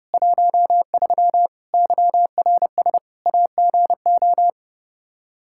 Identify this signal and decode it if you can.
Morse code, heard as a keyed tone, about 32 wpm, 700 Hz. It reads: 13 YRS AGO
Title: K1USN SST QSO Practice 30wpm